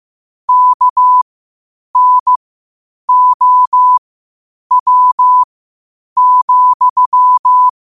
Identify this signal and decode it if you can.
Morse code, heard as a keyed tone, about 15 wpm, spaced as for 8 wpm, 1000 Hz. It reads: KNOW,
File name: qotdfiles / 8wpm